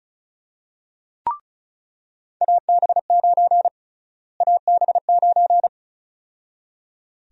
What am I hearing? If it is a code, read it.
Morse code, heard as a keyed tone, about 34 wpm, 700 Hz. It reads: AB9 AB9
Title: USA Call Sign Prefixes - Repeated 2x - 35wpm